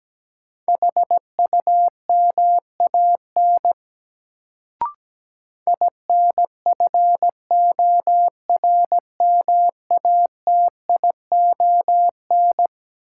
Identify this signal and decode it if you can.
Morse code, heard as a keyed tone, about 17 wpm, 700 Hz. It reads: HUMAN INFORMATION
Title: Sentences from Top 600 Words - New Word Drill - 17wpm